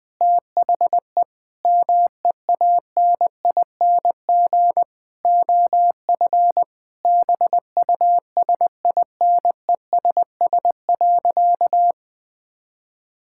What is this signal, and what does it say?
Morse code, heard as a keyed tone, about 19 wpm, 700 Hz. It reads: THE MEANING OF BUSINESS.
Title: Sentences from Top 1000 Words - Review Part 6 - 20wpm